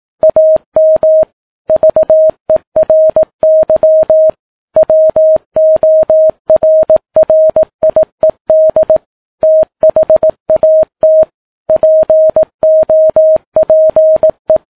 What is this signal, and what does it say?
Morse code, heard as a keyed tone, about 18 wpm, 630 Hz. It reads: AM VERY WORRIED THAT POPE
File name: Ham Radio Sat